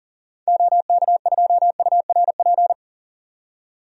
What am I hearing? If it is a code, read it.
Morse code, heard as a keyed tone, about 40 wpm, 700 Hz. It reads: OK2URP
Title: DXCC-100 Call Signs - 6 Characters - 40wpm